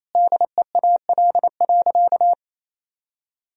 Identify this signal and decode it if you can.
Morse code, heard as a keyed tone, about 28 wpm, 700 Hz. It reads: DEAL.
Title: Sentences from Top 1000 Words - Review Part 4 - 28wpm